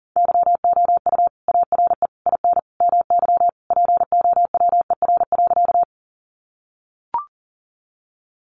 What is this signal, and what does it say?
Morse code, heard as a keyed tone, about 40 wpm, 700 Hz. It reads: YOU ARE IN MY POWER.
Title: Sentences from Top 500 Words 40wpm